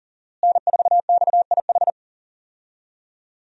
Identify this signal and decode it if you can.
Morse code, heard as a keyed tone, about 40 wpm, 700 Hz. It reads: N4XIH
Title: USA Call Signs - 5 Characters - Repeated 2x - 40wpm